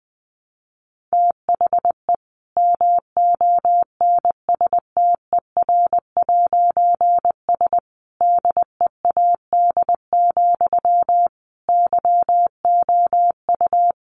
Morse code, heard as a keyed tone, about 20 wpm, 700 Hz. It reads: THE MONSTER'S DEAD, YOU
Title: A Sound Of Thunder 20wpm